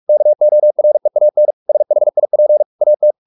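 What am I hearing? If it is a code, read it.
Morse code, heard as a keyed tone, about 44 wpm, 600 Hz. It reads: KOREAN SHIP AT
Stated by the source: Morse practice at 45 wpm, the novel Never by Ken Follett in English, from CW med Gnister.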